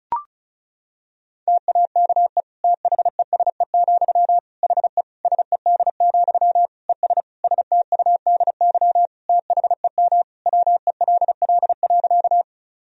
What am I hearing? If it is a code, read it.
Morse code, heard as a keyed tone, about 34 wpm, 700 Hz. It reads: TAKE THESE, HE SED, ES STUDY THEM WELL.
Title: Sentences from Top 300 Words and Abbreviations 35wpm